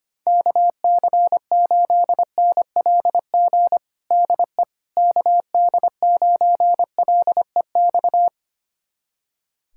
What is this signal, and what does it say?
Morse code, heard as a keyed tone, about 25 wpm, 700 Hz. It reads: KC8NLG DE KD9LEX
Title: USA Callsign DE Callsign - 6 Characters - 25wpm